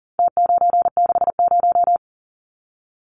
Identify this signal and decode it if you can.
Morse code, heard as a keyed tone, about 40 wpm, 700 Hz. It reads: T960
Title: Numbers - Contest Style - 4-Digit 40wpm